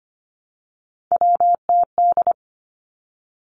Morse code, heard as a keyed tone, about 25 wpm, 700 Hz. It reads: WTD
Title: Three Challenging Letter-Number Combinations 25wpm